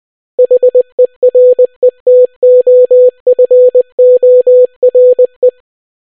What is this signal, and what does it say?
Morse code, heard as a keyed tone, about 20 wpm, 500 Hz. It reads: HERETOFORE